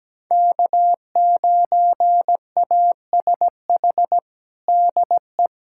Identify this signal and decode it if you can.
Morse code, heard as a keyed tone, about 17 wpm, 700 Hz. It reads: K9ASH DE
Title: USA Callsign DE Callsign - 5 Characters - 17wpm